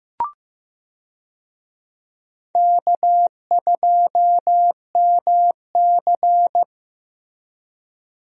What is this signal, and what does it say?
Morse code, heard as a keyed tone, about 15 wpm, 700 Hz. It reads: K2MC
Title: USA Call Signs - 4 Characters - Repeated 2x - 15wpm